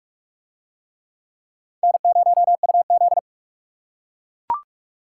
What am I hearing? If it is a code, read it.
Morse code, heard as a keyed tone, about 41 wpm, 690 Hz. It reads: N0UZ